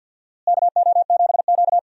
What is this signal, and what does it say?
Morse code, heard as a keyed tone, about 53 wpm, 700 Hz. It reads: KO7Q
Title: USA Call Signs - 4 Characters - Repeated 2x - 50wpm